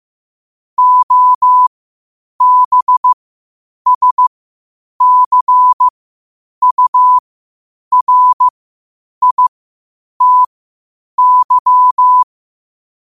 Morse code, heard as a keyed tone, about 15 wpm, spaced as for 8 wpm, 1000 Hz. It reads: OBSCURITY
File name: qotdfiles / 8wpm